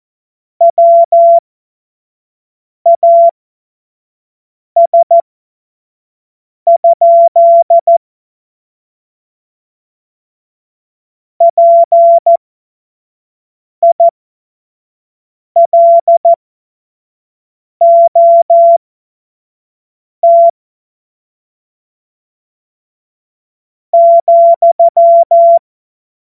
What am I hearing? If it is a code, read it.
Morse code, heard as a keyed tone, about 14 wpm, spaced as for 4 wpm, 680 Hz. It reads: WAS? PILOT,
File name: Morsekurs / Witze / Witz 30. Mai 2023 25 F70